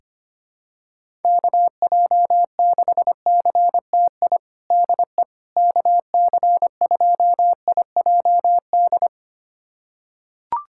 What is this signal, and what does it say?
Morse code, heard as a keyed tone, about 25 wpm, 700 Hz. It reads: KJ6CTI DE KC2IJD
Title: USA Callsign DE Callsign - 6 Characters - 25wpm